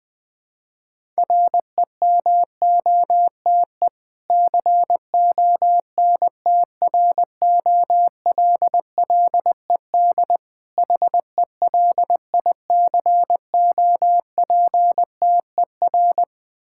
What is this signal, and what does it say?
Morse code, heard as a keyed tone, about 20 wpm, 690 Hz. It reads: REMOTE CONTROLLED HELICOPTER
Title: Difficult Words with Hints - 20wpm